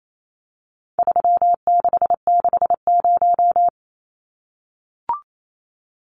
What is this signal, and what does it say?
Morse code, heard as a keyed tone, about 28 wpm, 700 Hz. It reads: 3660
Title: Numbers - 4-Digit 28wpm